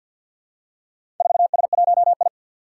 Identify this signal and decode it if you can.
Morse code, heard as a keyed tone, about 53 wpm, 700 Hz. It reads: 4S1I
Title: Four Challenging Letter-Number Combinations 50wpm